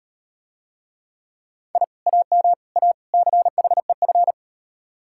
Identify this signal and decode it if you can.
Morse code, heard as a keyed tone, about 38 wpm, 700 Hz. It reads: I AM A CHEF